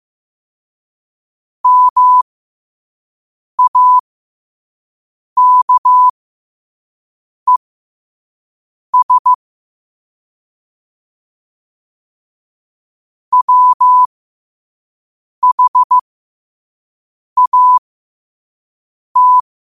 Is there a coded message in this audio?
Yes: Morse code from 1.6 to 19.4 s, heard as a keyed tone, about 15 wpm, spaced as for 5 wpm, 1000 Hz. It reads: MAKES WHAT